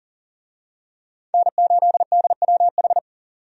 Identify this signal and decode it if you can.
Morse code, heard as a keyed tone, about 40 wpm, 700 Hz. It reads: N8DWH
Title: USA Call Signs - 5 Characters - 40wpm